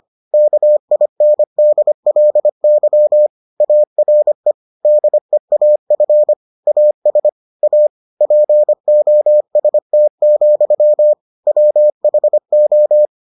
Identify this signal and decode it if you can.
Morse code, heard as a keyed tone, about 25 wpm, 600 Hz. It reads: KINDLY ARE DEAF AS A POST, WHO